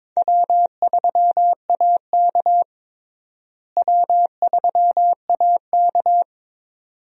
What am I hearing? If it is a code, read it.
Morse code, heard as a keyed tone, about 22 wpm, 700 Hz. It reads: W3AK W3AK
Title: USA Call Signs - 4 Characters - Repeated 2x - 22wpm